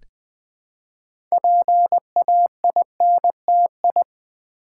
Morse code, heard as a keyed tone, about 20 wpm, 700 Hz. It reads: PAINTI
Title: Build A Word - 2 Letters At A Time - 8-Letter Words - 20wpm